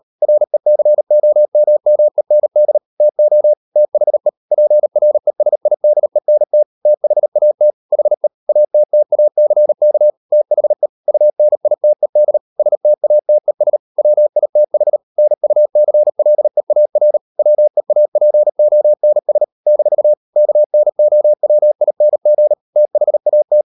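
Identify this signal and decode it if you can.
Morse code, heard as a keyed tone, about 38 wpm, 600 Hz. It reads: RECOMMEND TO THE PRESIDENT THAT HE ATTACK THE UNITED STATES WITH NUCLEAR WEAPONS - KNOWING THAT
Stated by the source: Morse code practice at 35–39 wpm sending English text of Ken Follett's novel Never, from CW med Gnister.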